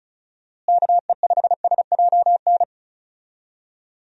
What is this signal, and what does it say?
Morse code, heard as a keyed tone, about 36 wpm, 700 Hz. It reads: KE5SJN
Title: USA Call Signs - 6 Characters - Repeated 3x - 35wpm